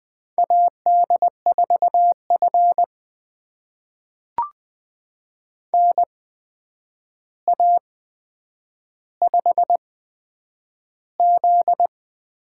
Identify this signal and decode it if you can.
Morse code, heard as a keyed tone, about 20 wpm, spaced as for 9 wpm, 700 Hz. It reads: AD4F NA5Z
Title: USA Call Signs - 4 Characters - 8x Character Spacing - 20wpm